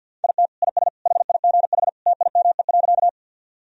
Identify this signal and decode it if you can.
Morse code, heard as a keyed tone, about 47 wpm, 700 Hz. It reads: IT IS HIGH TIME.
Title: Sentences from Top 1000 Words - Review Part 1 - 50wpm